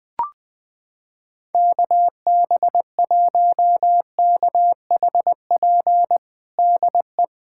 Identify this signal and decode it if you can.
Morse code, heard as a keyed tone, about 20 wpm, 700 Hz. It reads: KB1KHP DE
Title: USA Callsign DE Callsign 20wpm